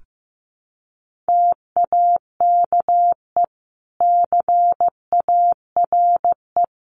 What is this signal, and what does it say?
Morse code, heard as a keyed tone, about 15 wpm, 700 Hz. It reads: TAKE CARE
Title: Sentences from Top 300 Words 15wpm